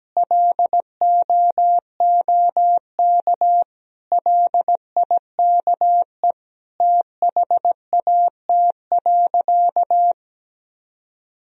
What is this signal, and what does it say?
Morse code, heard as a keyed tone, about 17 wpm, 700 Hz. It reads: LOOK LIKE THAT.
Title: Sentences from Top 1000 Words - Review Part 1 - 17wpm